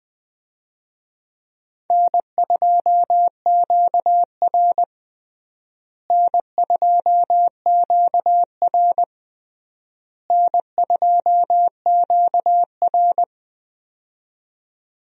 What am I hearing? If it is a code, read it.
Morse code, heard as a keyed tone, about 20 wpm, 700 Hz. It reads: N2QR N2QR N2QR
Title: USA Call Signs - 4 Characters - Repeated 3x - 20wpm